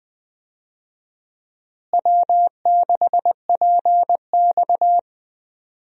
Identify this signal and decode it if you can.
Morse code, heard as a keyed tone, about 20 wpm, 700 Hz. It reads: W6PX
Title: USA Call Signs - 4 Characters - Repeated 3x - 20wpm